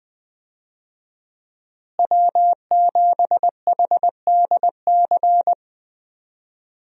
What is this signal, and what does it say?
Morse code, heard as a keyed tone, about 20 wpm, 700 Hz. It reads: W7HDC
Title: USA Call Signs - 5 Characters - Repeated 2x - 20wpm